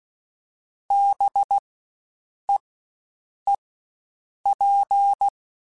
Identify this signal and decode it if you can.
Morse code, heard as a keyed tone, about 16 wpm, spaced as for 7 wpm, 800 Hz. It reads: BEEP
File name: Spaced Alphas / Spaced Alphas mp3s / Bs sp @15